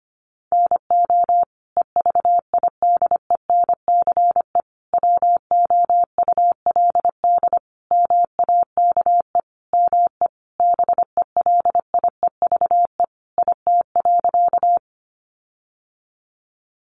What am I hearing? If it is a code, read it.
Morse code, heard as a keyed tone, about 25 wpm, 700 Hz. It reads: NO EVIDENCE WOULD MAKE ME BELIEVE IT.
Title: Sentences from Top 1300 Words 25wpm